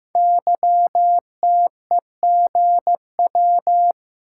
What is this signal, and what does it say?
Morse code, heard as a keyed tone, about 15 wpm, 700 Hz. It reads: YTEGW
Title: Code Groups - 5 Letter:Numbers - 15wpm